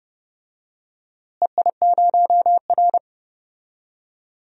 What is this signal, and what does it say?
Morse code, heard as a keyed tone, about 31 wpm, 700 Hz. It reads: EI0R
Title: DXCC-100 Call Signs - 4 Characters - Repeated 3x - 30wpm